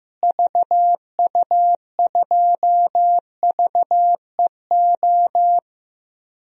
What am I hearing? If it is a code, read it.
Morse code, heard as a keyed tone, about 16 wpm, 700 Hz. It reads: VU2VEO
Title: DXCC-100 Call Signs - 6 Characters - Repeated 2x - 15wpm